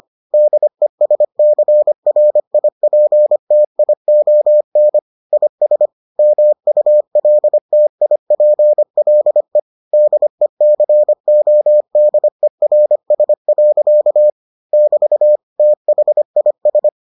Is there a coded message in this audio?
Yes: Morse code from 0.3 to 16.9 s, heard as a keyed tone, about 25 wpm, 600 Hz. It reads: DESCRIPTION IS MULTIPLE DECODERS. <BT> THIS